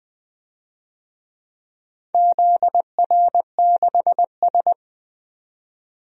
Morse code, heard as a keyed tone, about 20 wpm, 700 Hz. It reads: ZR6S
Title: DXCC-100 Call Signs - 4 Characters - 20wpm